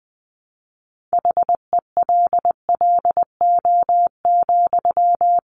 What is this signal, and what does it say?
Morse code, heard as a keyed tone, about 20 wpm, 700 Hz. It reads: HELLO,
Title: Sentences from Top 1100 Words 20wpm